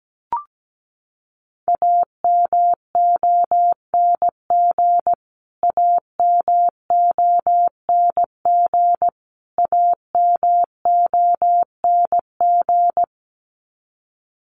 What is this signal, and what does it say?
Morse code, heard as a keyed tone, about 17 wpm, 700 Hz. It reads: AMONG AMONG AMONG
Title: Top 500 Words Repeated 17wpm